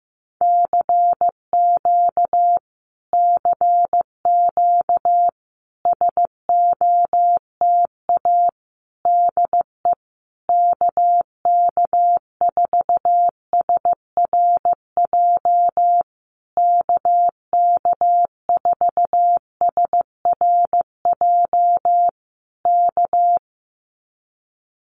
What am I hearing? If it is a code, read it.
Morse code, heard as a keyed tone, about 15 wpm, 700 Hz. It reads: CQ CQ SOTA DE KK4SRJ KK4SRJ K
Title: Common QSO Phrases Encore 15wpm